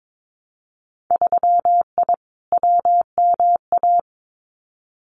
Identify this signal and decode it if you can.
Morse code, heard as a keyed tone, about 22 wpm, 700 Hz. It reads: 3I WMA
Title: Winter Field Day Exchanges 22wpm